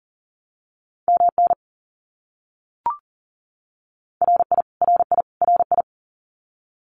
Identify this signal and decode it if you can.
Morse code, heard as a keyed tone, about 40 wpm, 700 Hz. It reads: MN RI RI RI
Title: US States Repeated 3x 40wpm